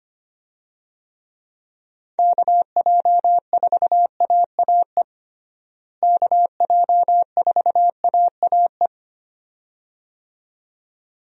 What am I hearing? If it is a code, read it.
Morse code, heard as a keyed tone, about 25 wpm, 700 Hz. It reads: KJ4AAE KJ4AAE
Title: USA Call Signs - 6 Characters - Repeated 2x - 25wpm